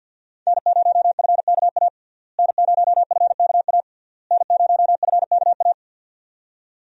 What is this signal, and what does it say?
Morse code, heard as a keyed tone, about 48 wpm, 700 Hz. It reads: N0FKA N0FKA N0FKA
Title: USA Call Signs - 5 Characters - Repeated 3x - 50wpm